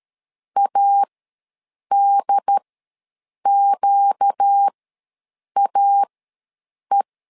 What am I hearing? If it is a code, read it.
Morse code, heard as a keyed tone, about 13 wpm, spaced as for 7 wpm, 800 Hz. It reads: ADQAE
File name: LEZIONI / 4 Q ws 12wpm 1912